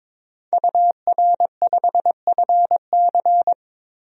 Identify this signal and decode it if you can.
Morse code, heard as a keyed tone, about 22 wpm, 700 Hz. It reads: UR5FC
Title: DXCC-100 Call Signs - 5 Characters - 22wpm